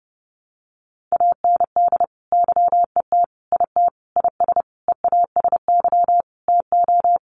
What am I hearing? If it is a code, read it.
Morse code, heard as a keyed tone, about 30 wpm, 700 Hz. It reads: AND YET IT IS EASY TO
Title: Sentences from Top 1400 Words 30wpm